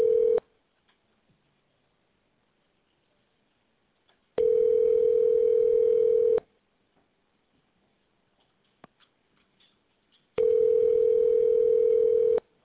Ringback tone (North America), 440 + 480 Hz.